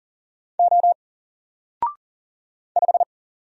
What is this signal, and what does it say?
Morse code, heard as a keyed tone, about 40 wpm, 700 Hz. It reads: O 5
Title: Single Character (Letter:Number:Punctuation:Prosign) 40wpm